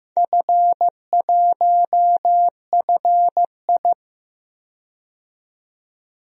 Morse code, heard as a keyed tone, about 15 wpm, 700 Hz. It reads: F1FI
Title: DXCC-100 Call Signs - 4 Characters - 15wpm